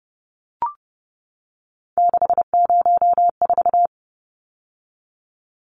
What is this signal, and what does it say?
Morse code, heard as a keyed tone, about 30 wpm, 700 Hz. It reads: 604
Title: Numbers - 3-Digit 30wpm